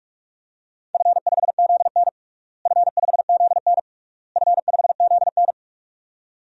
Morse code, heard as a keyed tone, about 46 wpm, 700 Hz. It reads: F5ZN F5ZN F5ZN
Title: DXCC-100 Call Signs - 4 Characters - Repeated 3x - 45wpm